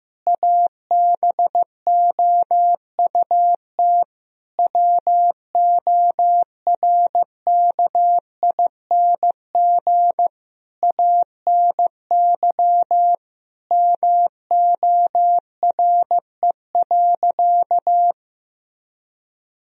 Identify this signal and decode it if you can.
Morse code, heard as a keyed tone, about 15 wpm, 700 Hz. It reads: ABOUT WORKING ANY MORE.
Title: Sentences from Top 1000 Words - Review Part 4 - 15wpm